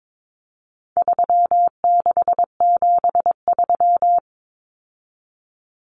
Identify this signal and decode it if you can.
Morse code, heard as a keyed tone, about 22 wpm, 700 Hz. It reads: 3673
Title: Common Cut Numbers - 4-Digit 22wpm